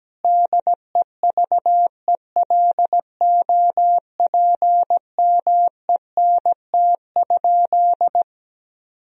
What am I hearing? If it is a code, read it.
Morse code, heard as a keyed tone, about 17 wpm, 700 Hz. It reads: DEVELOPMENT?
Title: Sentences from Top 1000 Words - Review Part 4 - 17wpm